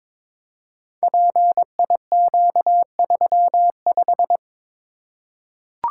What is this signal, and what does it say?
Morse code, heard as a keyed tone, about 22 wpm, 700 Hz. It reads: PIQ35
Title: Code Groups - 5 Letter:Numbers - 22wpm